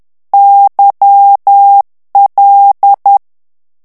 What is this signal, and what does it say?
Morse code, heard as a keyed tone, about 10 wpm, 800 Hz. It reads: YL
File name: morse mp3 / ITALIEN texte clair 1 vitesse 10 mots minute